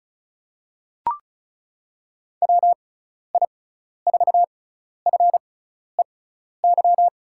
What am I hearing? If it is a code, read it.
Morse code, heard as a keyed tone, about 36 wpm, spaced as for 12 wpm, 690 Hz. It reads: WI4FEY
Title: USA Call Signs - 6 Characters - 6x Character Spacing - 35wpm